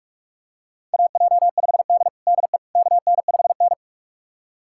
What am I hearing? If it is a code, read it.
Morse code, heard as a keyed tone, about 41 wpm, 700 Hz. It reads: AJ5D DE KN5N
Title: USA Callsign DE Callsign - 4 Characters - 45wpm